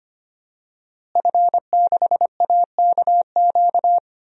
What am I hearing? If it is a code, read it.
Morse code, heard as a keyed tone, about 25 wpm, 700 Hz. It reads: F6AKQ
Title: DXCC-100 Call Signs - 5 Characters - Repeated 3x - 25wpm